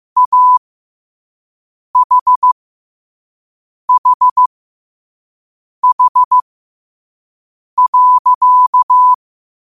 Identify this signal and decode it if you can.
Morse code, heard as a keyed tone, about 15 wpm, spaced as for 5 wpm, 1000 Hz. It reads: AHHH.
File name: qotdfiles / 5wpm